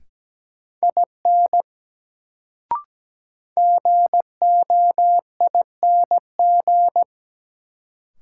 Morse code, heard as a keyed tone, about 17 wpm, 700 Hz. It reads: IN GOING
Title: Sets of 2 Words - New Word Drill - 17wpm